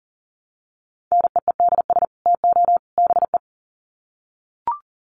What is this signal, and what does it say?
Morse code, heard as a keyed tone, about 40 wpm, 700 Hz. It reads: NEEDS TO BE
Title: Sets of 3 Words 40wpm